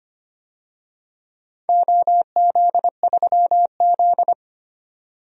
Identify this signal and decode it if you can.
Morse code, heard as a keyed tone, about 25 wpm, 700 Hz. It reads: OZ3Z
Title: DXCC-100 Call Signs - 4 Characters - 25wpm